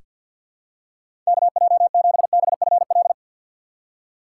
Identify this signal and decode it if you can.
Morse code, heard as a keyed tone, about 47 wpm, 700 Hz. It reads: KJ7BFL
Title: USA Call Signs - 6 Characters - 4x Character Spacing - 50wpm